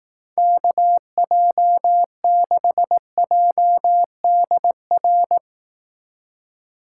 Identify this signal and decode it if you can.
Morse code, heard as a keyed tone, about 18 wpm, 700 Hz. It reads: KJ6JDR
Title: Speed Racing - USA 2x3 Call Signs 15wpm